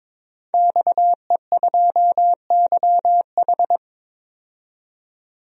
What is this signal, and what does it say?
Morse code, heard as a keyed tone, about 22 wpm, 700 Hz. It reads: XE2YH
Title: DXCC-100 Call Signs - 5 Characters - 22wpm